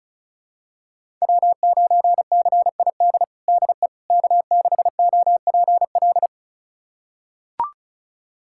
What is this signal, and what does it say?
Morse code, heard as a keyed tone, about 34 wpm, 700 Hz. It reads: W9CID DE K6OPL
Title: USA Callsign DE Callsign - 5 Characters - 35wpm